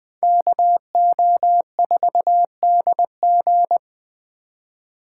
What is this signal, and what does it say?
Morse code, heard as a keyed tone, about 20 wpm, 700 Hz. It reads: KO4DG